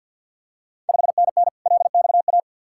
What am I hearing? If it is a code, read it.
Morse code, heard as a keyed tone, about 52 wpm, 700 Hz. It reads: 5NN LXA
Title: Ontario QSO Party Exchange 50wpm